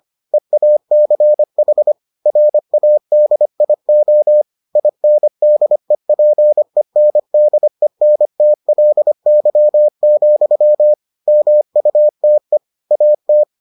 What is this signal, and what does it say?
Morse code, heard as a keyed tone, about 25 wpm, 600 Hz. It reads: EACH RADIO INDEPENDENTLY, MUTE AT